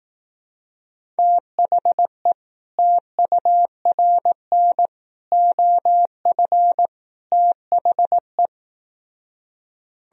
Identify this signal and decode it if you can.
Morse code, heard as a keyed tone, about 18 wpm, 700 Hz. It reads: THE TURN OF THE
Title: Speed-Racing - Sets of 4 Words 15wpm